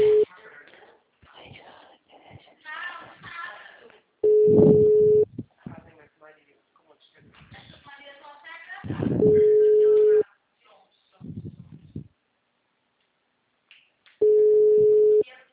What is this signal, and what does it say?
Ringback tone (Europe, 425 Hz), with speech.